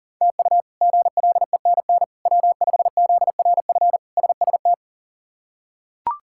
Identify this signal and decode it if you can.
Morse code, heard as a keyed tone, about 37 wpm, 700 Hz. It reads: TU GLENN W5ZRF SST
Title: K1USN SST QSO Practice - 40wpm